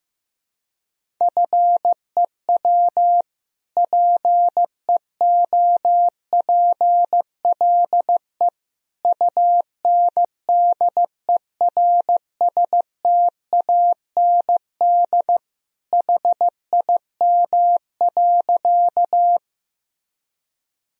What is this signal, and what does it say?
Morse code, heard as a keyed tone, about 15 wpm, 700 Hz. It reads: FEW PEOPLE UNDERSTAND HIM.
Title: Sentences from Top 1000 Words - Review Part 1 - 15wpm